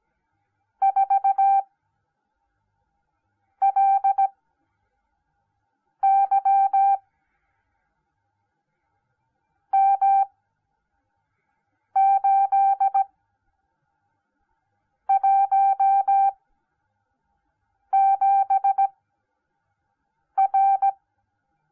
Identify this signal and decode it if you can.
Morse code, heard as a keyed tone, about 17 wpm, spaced as for 4 wpm, 780 Hz. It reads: 4LY M817R